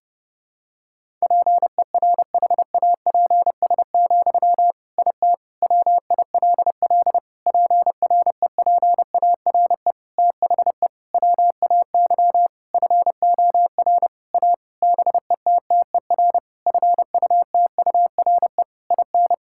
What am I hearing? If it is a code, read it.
Morse code, heard as a keyed tone, about 30 wpm, 700 Hz. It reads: PERHAPS, IT WILL PREPARE THE WAY FOR A BETTER FUTURE IN
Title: Sentences from Top 1000 Words - Review Part 6 - 30wpm